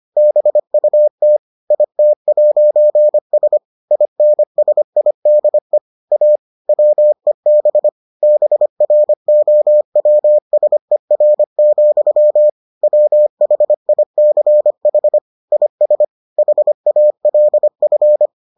Morse code, heard as a keyed tone, about 25 wpm, 600 Hz. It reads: BUT IT'S INSIDE A WEB BROWSER, WHICH IS HALF